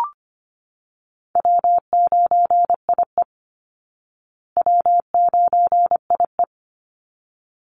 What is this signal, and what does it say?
Morse code, heard as a keyed tone, about 25 wpm, 700 Hz. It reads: W9IE W9IE
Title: Easy Call Signs Repeated 2x - 25wpm